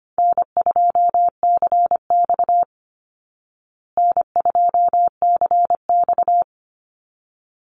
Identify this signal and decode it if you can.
Morse code, heard as a keyed tone, about 25 wpm, 700 Hz. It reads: N2CX N2CX
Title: Easy Call Signs Repeated 2x - 25wpm